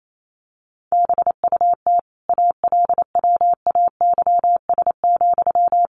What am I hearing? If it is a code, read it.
Morse code, heard as a keyed tone, about 28 wpm, 700 Hz. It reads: BUT ALWAYS,
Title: Sentences from Top 1200 Words 28wpm